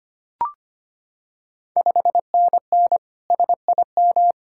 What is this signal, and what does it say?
Morse code, heard as a keyed tone, about 25 wpm, 700 Hz. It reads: 5NN SIM
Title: Ontario QSO Party Exchange 25wpm